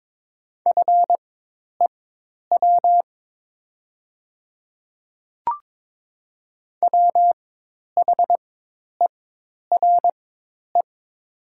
Morse code, heard as a keyed tone, about 22 wpm, spaced as for 10 wpm, 700 Hz. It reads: FEW WHERE
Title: 4x Character Spacing - Top 300 Words - 22wpm